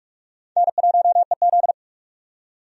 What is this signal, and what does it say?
Morse code, heard as a keyed tone, about 46 wpm, 700 Hz. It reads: N1EZ
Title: USA Call Signs - 4 Characters - 45wpm